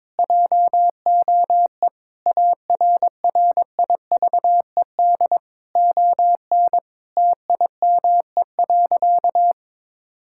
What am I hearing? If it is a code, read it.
Morse code, heard as a keyed tone, about 22 wpm, 700 Hz. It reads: JOE ARRIVED ON TIME.
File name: practice / 22wpm / Sentences with 4-Word Phrases - 22wpm